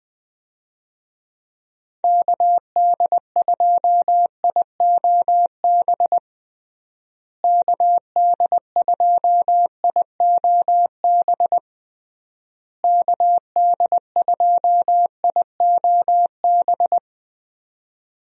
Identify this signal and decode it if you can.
Morse code, heard as a keyed tone, about 20 wpm, 700 Hz. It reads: KD2IOB KD2IOB KD2IOB
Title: USA Call Signs - 6 Characters - Repeated 3x - 20wpm